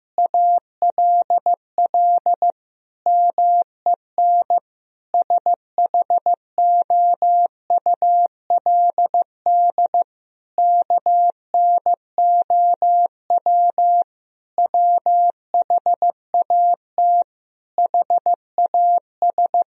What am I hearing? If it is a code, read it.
Morse code, heard as a keyed tone, about 15 wpm, 700 Hz. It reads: ALL MEN SHOULD KNOW WHAT HAS
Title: Sentences from Top 1000 Words - Review Part 1 - 15wpm